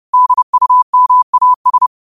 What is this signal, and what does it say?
Morse code, heard as a keyed tone, about 30 wpm, 1000 Hz. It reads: DUMAS